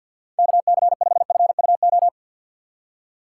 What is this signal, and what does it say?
Morse code, heard as a keyed tone, about 53 wpm, 700 Hz. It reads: KC5FUO
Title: USA Call Signs - 6 Characters - 6x Character Spacing - 50wpm